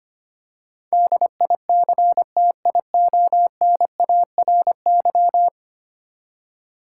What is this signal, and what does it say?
Morse code, heard as a keyed tone, about 25 wpm, 700 Hz. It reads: DICTIONARY
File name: Ten Letter Words - 25wpm